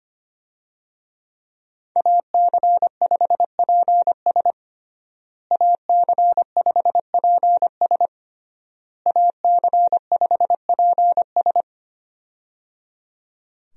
Morse code, heard as a keyed tone, about 25 wpm, 700 Hz. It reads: AC5PS AC5PS AC5PS
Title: USA Call Signs - 5 Characters - Repeated 3x - 25wpm